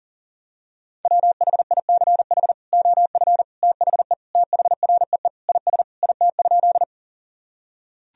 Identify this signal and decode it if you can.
Morse code, heard as a keyed tone, about 40 wpm, 700 Hz. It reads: WHICH OF THE THREE IS IT?
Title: Sentences from Top 1000 Words - Review Part 1 - 40wpm